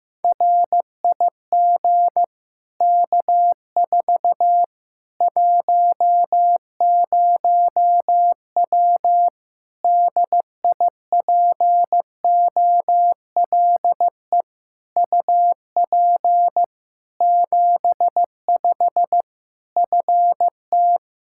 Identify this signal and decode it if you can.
Morse code, heard as a keyed tone, about 15 wpm, 700 Hz. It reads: RIG K4 10W DIPOLE UP 75 FT
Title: Practice QSO - Exchange 2 - Info - 15wpm